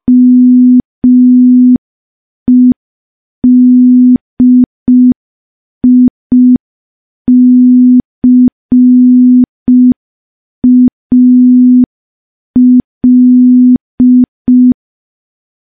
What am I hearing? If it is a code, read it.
Morse code, heard as a keyed tone, about 5 wpm, 250 Hz. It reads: MEDICAL
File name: morse / morse-training-05WPM